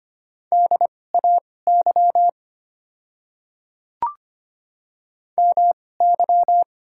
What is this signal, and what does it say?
Morse code, heard as a keyed tone, about 25 wpm, spaced as for 18 wpm, 700 Hz. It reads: DAY MY
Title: 2x Character Spacing - Top 100 Words - 25wpm